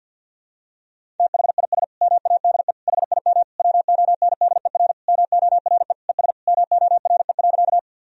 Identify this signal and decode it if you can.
Morse code, heard as a keyed tone, about 50 wpm, 700 Hz. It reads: THIS MADE HIM WONDER MORE ES MORE.
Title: Sentences from Top 500 Words and Abbreviations 50wpm